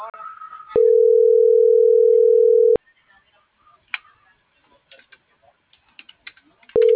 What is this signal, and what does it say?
Ringback tone (North America), 440 + 480 Hz.